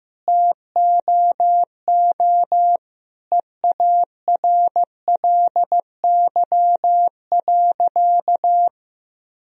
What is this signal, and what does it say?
Morse code, heard as a keyed tone, about 16 wpm, 700 Hz. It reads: TOO EARLY.
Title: Sentences from Top 1000 Words - Review Part 2 - 15wpm